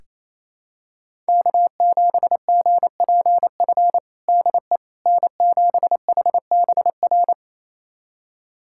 Morse code, heard as a keyed tone, about 28 wpm, 700 Hz. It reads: K7GPF DE N7HBR
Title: USA Callsign DE Callsign - 5 Characters - 28wpm